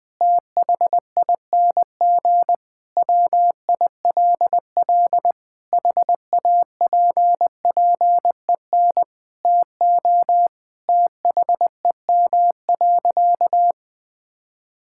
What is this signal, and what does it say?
Morse code, heard as a keyed tone, about 19 wpm, 700 Hz. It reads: THING WILL HAPPEN TO THEM.
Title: Sentences from Top 1000 Words - Review Part 5 - 20wpm